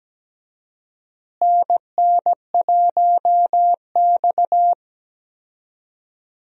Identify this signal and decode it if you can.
Morse code, heard as a keyed tone, about 17 wpm, 700 Hz. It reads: NN1X
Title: USA Call Signs - 4 Characters - Repeated 2x - 17wpm